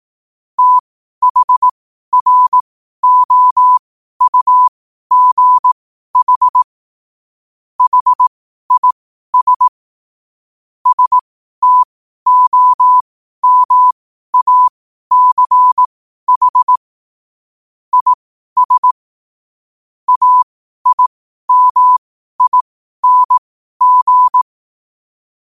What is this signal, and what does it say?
Morse code, heard as a keyed tone, about 18 wpm, spaced as for 12 wpm, 1000 Hz. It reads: THROUGH HIS STOMACH IS AIMING
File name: qotdfiles / 12wpm